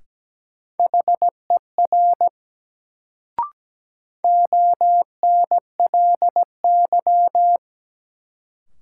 Morse code, heard as a keyed tone, about 17 wpm, 700 Hz. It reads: HER ONLY
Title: Sets of 2 Words - New Word Drill - 17wpm